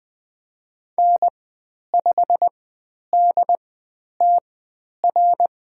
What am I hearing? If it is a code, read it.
Morse code, heard as a keyed tone, about 20 wpm, spaced as for 10 wpm, 700 Hz. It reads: N5DTR
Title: 102 Lesson 17g - Letter u - Call Signs 20wpm